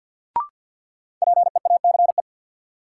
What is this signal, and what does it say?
Morse code, heard as a keyed tone, about 53 wpm, 700 Hz. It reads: PEACE
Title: Sentences from Top 1500 Words - New Word Drill - 50wpm